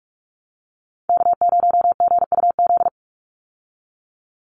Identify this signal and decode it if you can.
Morse code, heard as a keyed tone, about 44 wpm, 700 Hz. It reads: K0GUZ
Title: Call Signs Repeated 2x - 45wpm